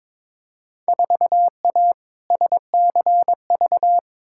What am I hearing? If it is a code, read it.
Morse code, heard as a keyed tone, about 22 wpm, 700 Hz. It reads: 4A SCV
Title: Field Day Exchanges Encore - 22wpm